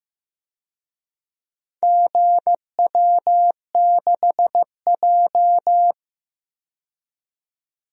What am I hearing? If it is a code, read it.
Morse code, heard as a keyed tone, about 15 wpm, 700 Hz. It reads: GW6J
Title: DXCC-100 Call Signs - 4 Characters - 15wpm